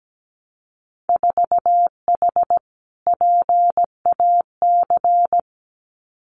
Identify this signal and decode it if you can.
Morse code, heard as a keyed tone, about 17 wpm, 700 Hz. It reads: 4H PAC
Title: Winter Field Day Exchanges Encore 17wpm